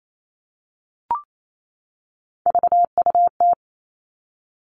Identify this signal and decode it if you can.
Morse code, heard as a keyed tone, about 28 wpm, 700 Hz. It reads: VUT
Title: All Cut Numbers - 3-Digit 28wpm